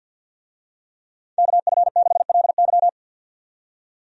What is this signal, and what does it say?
Morse code, heard as a keyed tone, about 50 wpm, 700 Hz. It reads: KF6LY
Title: USA Call Signs - 5 Characters - 50wpm